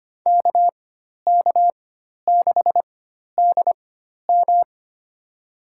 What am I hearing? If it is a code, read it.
Morse code, heard as a keyed tone, about 25 wpm, spaced as for 12 wpm, 700 Hz. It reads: KK6DM
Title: USA Call Signs - 4x Character Spacing - 25wpm